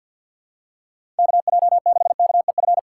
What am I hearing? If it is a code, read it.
Morse code, heard as a keyed tone, about 51 wpm, 690 Hz. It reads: KJ6KEF
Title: USA Call Signs - 6 Characters - 3x Character Spacing - 50wpm